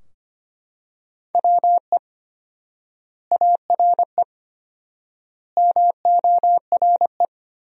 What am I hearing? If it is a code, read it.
Morse code, heard as a keyed tone, about 25 wpm, 700 Hz. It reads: WE ARE MORE
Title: 4x Word-Spacing - Sentences from Top 500 Words - 25wpm